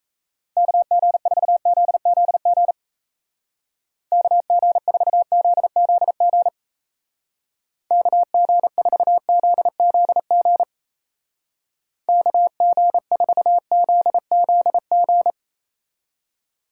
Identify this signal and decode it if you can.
Morse code, heard as a keyed tone, about 35 wpm, 700 Hz. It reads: KG4ZZG KG4ZZG KG4ZZG KG4ZZG